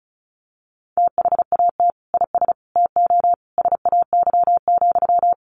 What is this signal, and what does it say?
Morse code, heard as a keyed tone, about 34 wpm, 700 Hz. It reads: THAT IS TO SAY,
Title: Sentences from Top 700 Words 35wpm